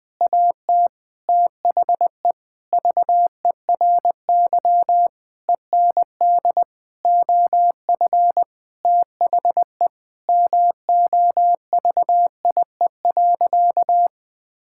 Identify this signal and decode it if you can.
Morse code, heard as a keyed tone, about 20 wpm, 700 Hz. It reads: AT THE VERY END OF THE MOVIE.
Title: Sentences with 6-Word Phrases - 20wpm